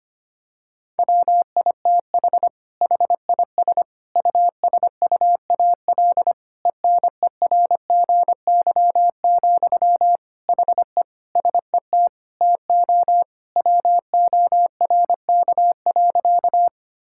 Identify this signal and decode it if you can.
Morse code, heard as a keyed tone, about 25 wpm, 700 Hz. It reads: WITH HIS USUAL ENERGY, HE SET TO WORK.
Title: Sentences from Top 1000 Words - Review Part 6 - 25wpm